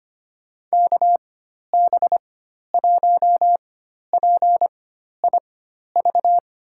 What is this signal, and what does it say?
Morse code, heard as a keyed tone, about 25 wpm, spaced as for 12 wpm, 700 Hz. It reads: KB1PIV
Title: USA Call Signs - 6 Characters - 4x Character Spacing - 25wpm